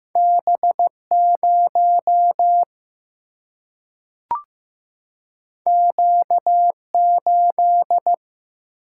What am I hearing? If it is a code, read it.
Morse code, heard as a keyed tone, about 15 wpm, 700 Hz. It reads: B 0 Q 8
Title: Single-Letter Single-Number 15wpm